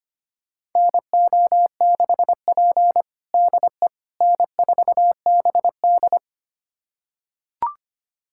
Morse code, heard as a keyed tone, about 25 wpm, 700 Hz. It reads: NO6P DE N4BD
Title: USA Callsign DE Callsign - 4 Characters - 25wpm